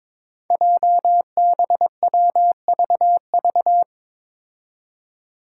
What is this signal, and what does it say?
Morse code, heard as a keyed tone, about 22 wpm, 700 Hz. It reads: JBWVV
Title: Five Challenging Letter-Number Combinations 22wpm